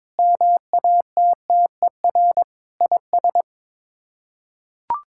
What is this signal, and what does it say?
Morse code, heard as a keyed tone, about 22 wpm, 700 Hz. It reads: MATTER IS
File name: practice / 22wpm / Sets of 6 Words - 22wpm